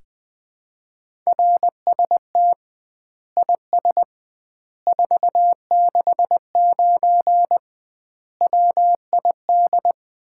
Morse code, heard as a keyed tone, about 20 wpm, 700 Hz. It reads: RST IS 469 WID
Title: Common QSO Phrases - 2x Word-Spacing - 20wpm